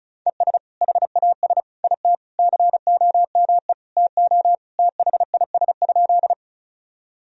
Morse code, heard as a keyed tone, about 36 wpm, 700 Hz. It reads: ES HAS IT COME TO THIS?
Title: Sentences from Top 100 Words and Abbreviations 35wpm